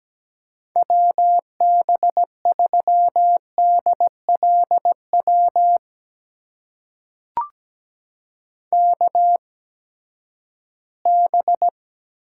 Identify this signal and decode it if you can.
Morse code, heard as a keyed tone, about 17 wpm, spaced as for 12 wpm, 700 Hz. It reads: WB3DLW KB
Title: USA Call Signs - 6 Characters - 8x Character Spacing - 17wpm